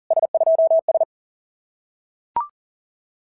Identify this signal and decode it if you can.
Morse code, heard as a keyed tone, about 40 wpm, 650 Hz. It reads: S2S
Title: SOTA QSO Practice 40wpm